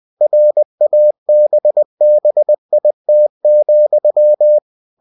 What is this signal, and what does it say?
Morse code, heard as a keyed tone, about 20 wpm, 600 Hz. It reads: RABBIT,
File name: cw-books / winnie-the-pooh / winnie-20wpm-page-0079